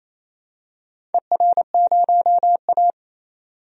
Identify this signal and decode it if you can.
Morse code, heard as a keyed tone, about 28 wpm, 700 Hz. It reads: ER0A
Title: DXCC-100 Call Signs - 4 Characters - 28wpm